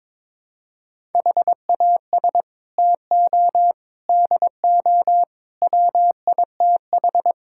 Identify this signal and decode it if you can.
Morse code, heard as a keyed tone, about 22 wpm, 700 Hz. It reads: HAS TO DO WITH
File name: practice / 22wpm / Build A Phrase - 5-Word Phrases - 22wpm